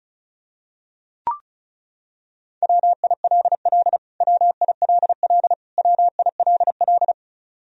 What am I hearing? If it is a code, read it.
Morse code, heard as a keyed tone, about 36 wpm, 700 Hz. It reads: WILL WILL WILL
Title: Top 100 Words Repeated 35wpm